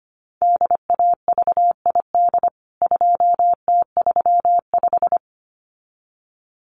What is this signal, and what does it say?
Morse code, heard as a keyed tone, about 25 wpm, 700 Hz. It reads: DAVID 2T35
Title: CWT Exchange 25wpm